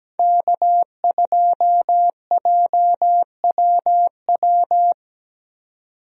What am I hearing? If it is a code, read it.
Morse code, heard as a keyed tone, about 17 wpm, 700 Hz. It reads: K2JWW
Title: USA Call Signs - 5 Characters - 17wpm